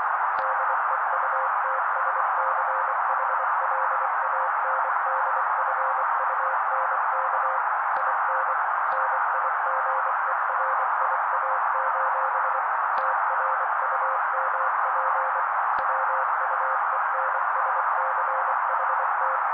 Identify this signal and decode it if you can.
Morse code, heard as a keyed tone, about 24 wpm, 530 Hz. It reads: DEUTSCHLANDFUNK IN NIGERIA: TRUMP WUENSCHT
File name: daily / 23wpm noise